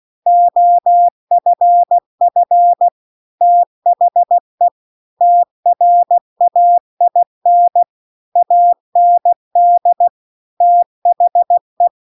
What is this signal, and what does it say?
Morse code, heard as a keyed tone, about 16 wpm, 700 Hz. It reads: OFF THE TRAIN AND THE